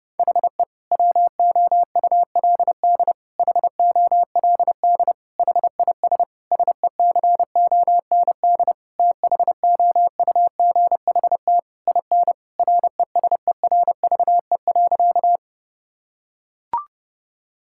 Morse code, heard as a keyed tone, about 32 wpm, 700 Hz. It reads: HE WOULD HOLD HIS SECOND THOUGHT IN RESERVE.
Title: Sentences from Top 1000 Words - Review Part 6 - 30wpm